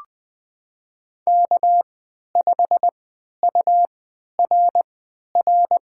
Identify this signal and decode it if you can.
Morse code, heard as a keyed tone, about 20 wpm, spaced as for 11 wpm, 700 Hz. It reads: K5URR
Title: USA Call Signs - 5 Characters - 3x Character Spacing - 20wpm